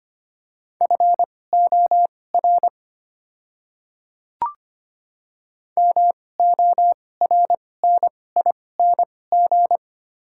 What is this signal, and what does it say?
Morse code, heard as a keyed tone, about 25 wpm, spaced as for 18 wpm, 700 Hz. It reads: FOR MORNING
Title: 2x Character Spacing - Top 500 Words - 25wpm